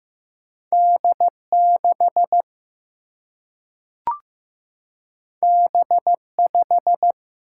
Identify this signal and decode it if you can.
Morse code, heard as a keyed tone, about 15 wpm, 700 Hz. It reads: D6 B5
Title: Single-Letter Single-Number 15wpm